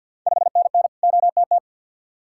Morse code, heard as a keyed tone, about 52 wpm, 690 Hz. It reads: 5NN OTT
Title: Ontario QSO Party Exchange 50wpm